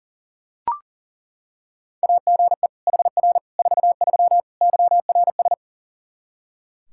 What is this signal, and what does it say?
Morse code, heard as a keyed tone, about 40 wpm, 700 Hz. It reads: AGE HR 43 YRS
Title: Practice QSO - Exchange 3 - More Info - 40wpm